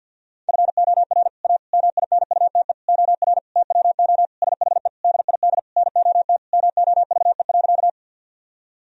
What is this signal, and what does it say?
Morse code, heard as a keyed tone, about 50 wpm, 700 Hz. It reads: FOR A MINUTE OR TWO SHE DID NOT MOVE.
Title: Sentences from Top 1000 Words - Review Part 1 - 50wpm